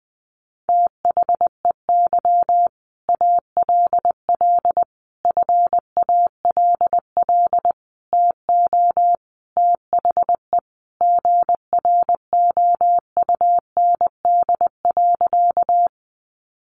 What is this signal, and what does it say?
Morse code, heard as a keyed tone, about 20 wpm, 700 Hz. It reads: THEY ALL FALL TO THE GROUND.
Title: Sentences from Top 500 Words 20wpm